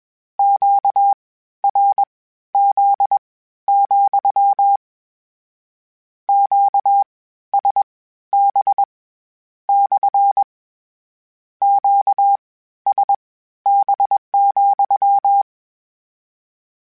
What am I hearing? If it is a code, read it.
Morse code, heard as a keyed tone, about 21 wpm, spaced as for 13 wpm, 800 Hz. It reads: QRZ, QSB / QSB,
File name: Q Calls / Q Calls-mp3s / Q CALLS SPSL @20